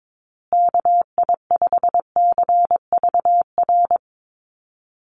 Morse code, heard as a keyed tone, about 22 wpm, 700 Hz. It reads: KI5CVR
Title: USA Call Signs Repeated 2x 22wpm